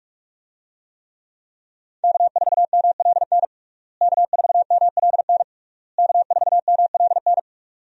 Morse code, heard as a keyed tone, about 46 wpm, 700 Hz. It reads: K4MLN K4MLN K4MLN
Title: USA Call Signs - 5 Characters - Repeated 3x - 45wpm